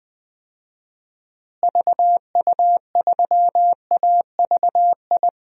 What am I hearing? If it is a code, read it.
Morse code, heard as a keyed tone, about 20 wpm, 700 Hz. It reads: VU3AVI
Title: DXCC-100 Call Signs - 6 Characters - Repeated 2x - 20wpm